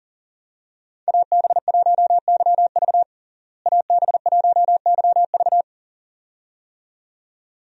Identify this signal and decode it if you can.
Morse code, heard as a keyed tone, about 40 wpm, 700 Hz. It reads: AB1YV AB1YV
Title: USA Call Signs - 5 Characters - Repeated 2x - 40wpm